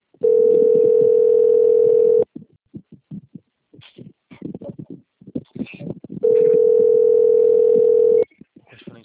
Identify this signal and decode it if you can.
Ringback tone (North America), 440 + 480 Hz.
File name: out- - - - - .79720